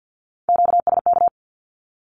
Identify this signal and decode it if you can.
Morse code, heard as a keyed tone, about 48 wpm, 700 Hz. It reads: QSK
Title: Q-Codes Repeated 3x 50wpm